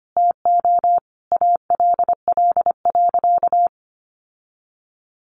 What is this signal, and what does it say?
Morse code, heard as a keyed tone, about 25 wpm, 700 Hz. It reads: TO ALL.
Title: Sentences from Top 1500 Words 25wpm